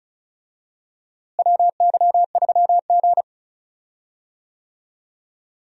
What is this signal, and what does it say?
Morse code, heard as a keyed tone, about 34 wpm, 700 Hz. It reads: WY3G
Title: USA Call Signs - 4 Characters - Repeated 3x - 35wpm